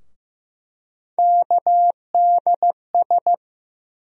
Morse code, heard as a keyed tone, about 15 wpm, 700 Hz. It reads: KDS
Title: USA Call Sign Postfixes - 3 Letters 15wpm